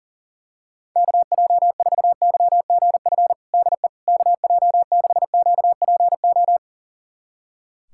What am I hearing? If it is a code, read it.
Morse code, heard as a keyed tone, about 37 wpm, 700 Hz. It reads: KJ4YGF DE KJ6QPO
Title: USA Callsign DE Callsign 40wpm